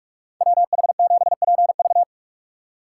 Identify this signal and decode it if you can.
Morse code, heard as a keyed tone, about 44 wpm, 700 Hz. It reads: WH7PV